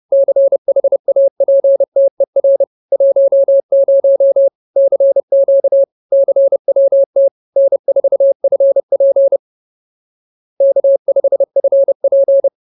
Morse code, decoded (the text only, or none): CHAPTER 10 CQ CWT N4FP K5FP